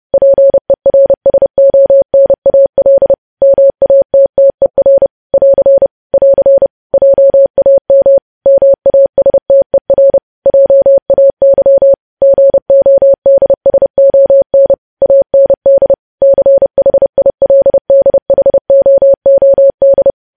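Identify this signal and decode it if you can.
Morse code, heard as a keyed tone, about 30 wpm, 550 Hz. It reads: PERSONAL MATTER <AR> <AR> JAM MASTER JAY GODSON AND CHILDHOOD
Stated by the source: Morse code news headlines sent at 30, preceded by vvv vvv.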